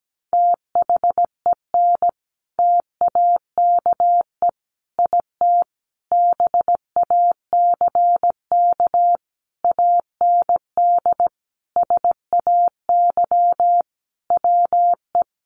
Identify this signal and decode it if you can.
Morse code, heard as a keyed tone, about 17 wpm, 700 Hz. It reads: THEN TAKE IT BACK AND SAY WE
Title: Sentences from Top 1500 Words 17wpm